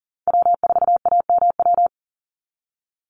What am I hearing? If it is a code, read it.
Morse code, heard as a keyed tone, about 40 wpm, 700 Hz. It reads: W4AMW
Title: NA SOTA Chaser Callsigns 40wpm